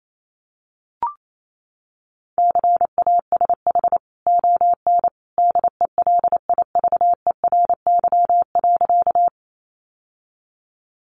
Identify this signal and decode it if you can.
Morse code, heard as a keyed tone, about 28 wpm, 700 Hz. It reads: CASH ON DELIVERY.
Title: Sentences from Top 1500 Words 28wpm